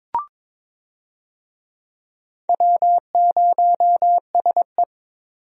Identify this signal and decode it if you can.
Morse code, heard as a keyed tone, about 22 wpm, 700 Hz. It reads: W0SE
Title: USA Call Signs - 4 Characters - 22wpm